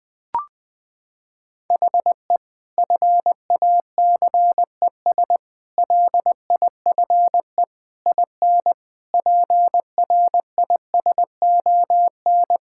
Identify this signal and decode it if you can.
Morse code, heard as a keyed tone, about 20 wpm, 700 Hz. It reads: HE FACES LIFE IN PRISON
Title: Sentences with 8-Word Phrases - 20wpm